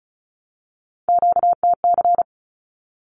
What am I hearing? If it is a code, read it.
Morse code, heard as a keyed tone, about 34 wpm, 700 Hz. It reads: QTC
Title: Q-Codes 35wpm